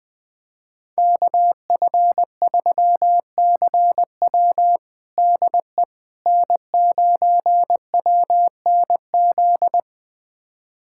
Morse code, heard as a keyed tone, about 21 wpm, 700 Hz. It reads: KF3CW DE N9WNZ
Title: USA Callsign DE Callsign - 5 Characters - 20wpm